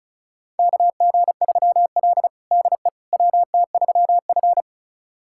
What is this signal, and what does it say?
Morse code, heard as a keyed tone, about 36 wpm, 700 Hz. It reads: KG3L DE WT3F
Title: USA Callsign DE Callsign - 4 Characters - 35wpm